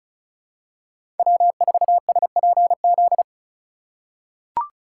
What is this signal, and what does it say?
Morse code, heard as a keyed tone, about 34 wpm, 700 Hz. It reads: W4SPZ
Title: USA Call Signs - 4x Character Spacing - 35wpm